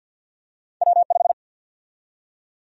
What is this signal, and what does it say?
Morse code, heard as a keyed tone, about 52 wpm, 700 Hz. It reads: W5
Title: USA Call Sign Prefixes - Repeated 2x - 50wpm